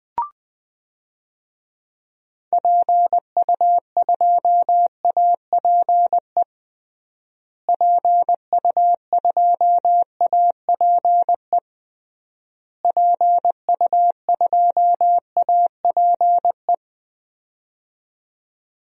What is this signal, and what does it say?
Morse code, heard as a keyed tone, about 20 wpm, 700 Hz. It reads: PU2APE PU2APE PU2APE
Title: DXCC-100 Call Signs - 6 Characters - Repeated 3x - 20wpm